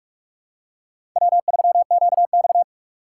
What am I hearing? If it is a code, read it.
Morse code, heard as a keyed tone, about 46 wpm, 700 Hz. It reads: W3QX
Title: USA Call Signs - 4 Characters - 4x Character Spacing - 45wpm